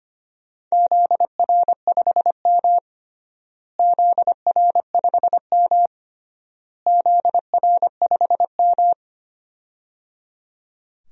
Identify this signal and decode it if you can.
Morse code, heard as a keyed tone, about 25 wpm, 700 Hz. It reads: ZR5M ZR5M ZR5M
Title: DXCC-100 Call Signs - 4 Characters - Repeated 3x - 25wpm